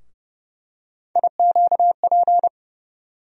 Morse code, heard as a keyed tone, about 30 wpm, 700 Hz. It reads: IQP
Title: USA Call Sign Postfixes - 3 Letters 30wpm